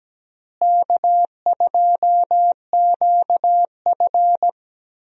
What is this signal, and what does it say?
Morse code, heard as a keyed tone, about 17 wpm, 700 Hz. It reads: K2QF
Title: USA Call Signs - 4 Characters - Repeated 2x - 17wpm